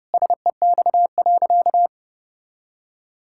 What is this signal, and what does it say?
Morse code, heard as a keyed tone, about 30 wpm, 700 Hz. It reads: SEX.
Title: Sentences from Top 1600 Words 30wpm